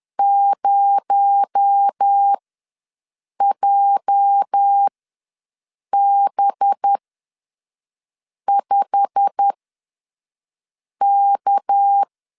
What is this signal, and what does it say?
Morse code, heard as a keyed tone, about 10 wpm, spaced as for 5 wpm, 800 Hz. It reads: 0JB5K
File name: LEZIONI / 8 YJ45 / 8 YJ45 ws 10wpm 3843